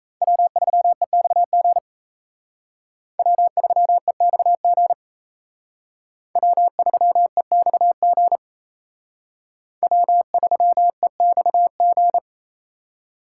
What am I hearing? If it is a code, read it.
Morse code, heard as a keyed tone, about 32 wpm, 700 Hz. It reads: W3EXG W3EXG W3EXG W3EXG
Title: Speed Racing - USA 1x3 Call Signs 28wpm